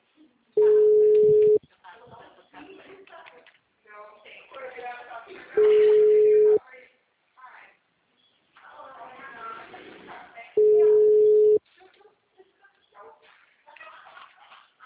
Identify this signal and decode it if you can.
Ringback tone (Europe, 425 Hz), with speech.
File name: out- -2032- - - .312